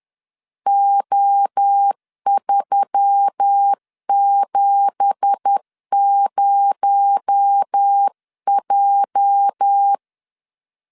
Morse code, heard as a keyed tone, about 10 wpm, 800 Hz. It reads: O370J
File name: LEZIONI / 10 SIGNS / 10B INT 10wpm 2844